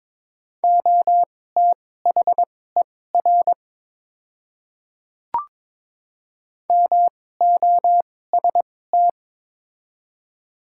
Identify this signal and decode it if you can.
Morse code, heard as a keyed tone, about 22 wpm, spaced as for 16 wpm, 700 Hz. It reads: OTHER MOST
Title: 2x Character Spacing - Top 200 Words - 22wpm